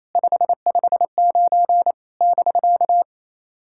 Morse code, heard as a keyed tone, about 28 wpm, 690 Hz. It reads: 559 <BK>
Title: SOTA QSO Practice 28wpm